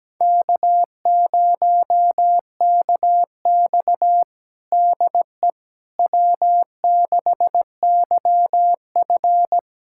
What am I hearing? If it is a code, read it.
Morse code, heard as a keyed tone, about 17 wpm, 700 Hz. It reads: K0KX DE W6YF
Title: USA Callsign DE Callsign - 4 Characters - 17wpm